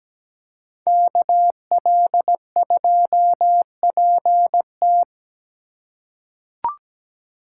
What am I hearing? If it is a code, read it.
Morse code, heard as a keyed tone, about 17 wpm, 700 Hz. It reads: KL2PT
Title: USA Call Signs - 5 Characters - 2x Character Spacing - 17wpm